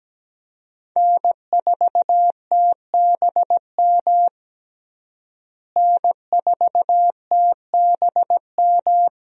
Morse code, heard as a keyed tone, about 17 wpm, 700 Hz. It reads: N4TBM N4TBM
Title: USA Call Signs - 5 Characters - Repeated 3x - 17wpm